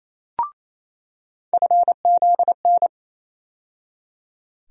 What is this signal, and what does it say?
Morse code, heard as a keyed tone, about 28 wpm, 700 Hz. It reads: FZN
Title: USA Call Sign Postfixes - 3 Letters 28wpm